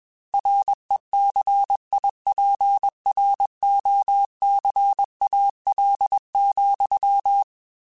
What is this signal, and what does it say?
Morse code, heard as a keyed tone, about 21 wpm, 800 Hz. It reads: RECIPROCAL,
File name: Technical Subjects / Technical Subjects-mp3s / XC @20